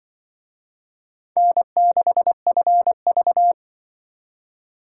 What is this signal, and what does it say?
Morse code, heard as a keyed tone, about 25 wpm, 700 Hz. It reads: N6FV
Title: Speed Racing - USA 1x2 Call Signs 20wpm